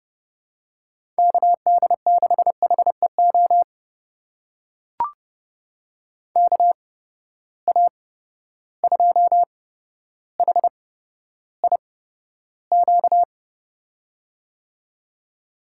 Morse code, heard as a keyed tone, about 30 wpm, spaced as for 13 wpm, 700 Hz. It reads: KD6HEO KA2HIQ
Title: USA Call Signs - 6 Characters - 8x Character Spacing - 30wpm